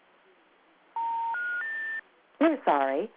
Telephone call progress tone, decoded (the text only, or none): special information tone